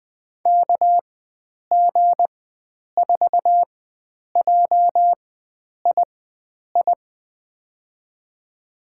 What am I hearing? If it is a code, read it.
Morse code, heard as a keyed tone, about 20 wpm, spaced as for 9 wpm, 700 Hz. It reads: KG4JII
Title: USA Call Signs - 6 Characters - 4x Character Spacing - 20wpm